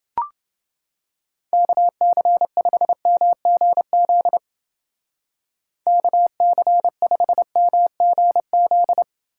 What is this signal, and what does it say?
Morse code, heard as a keyed tone, about 30 wpm, 700 Hz. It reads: KC5MGZ KC5MGZ